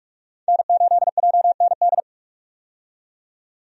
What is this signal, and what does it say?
Morse code, heard as a keyed tone, about 43 wpm, 700 Hz. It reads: N8JND
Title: USA Call Signs - 5 Characters - 45wpm